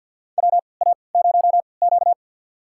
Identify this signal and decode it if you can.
Morse code, heard as a keyed tone, about 52 wpm, spaced as for 29 wpm, 700 Hz. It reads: WA0Q
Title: USA Call Signs - 4 Characters - 3x Character Spacing - 50wpm